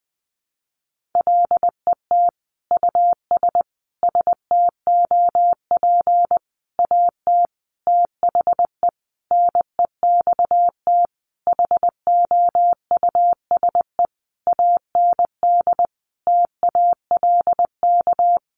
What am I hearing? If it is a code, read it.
Morse code, heard as a keyed tone, about 20 wpm, 700 Hz. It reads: LET US STOP AT THE NEXT HOUSE AND TALK
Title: Sentences from Top 400 Words 20wpm